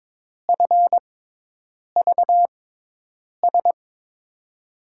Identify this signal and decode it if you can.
Morse code, heard as a keyed tone, about 22 wpm, spaced as for 8 wpm, 700 Hz. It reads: FVS
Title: USA Call Signs - 6 Characters - 6x Character Spacing - 22wpm